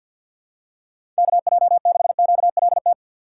Morse code, heard as a keyed tone, about 50 wpm, 700 Hz. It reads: KJ6QLT
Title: USA Call Signs - 6 Characters - 6x Character Spacing - 50wpm